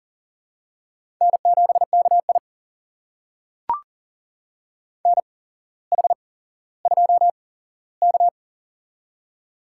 Morse code, heard as a keyed tone, about 40 wpm, spaced as for 17 wpm, 700 Hz. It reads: N7KI NH2K
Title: USA Call Signs - 4 Characters - 8x Character Spacing - 40wpm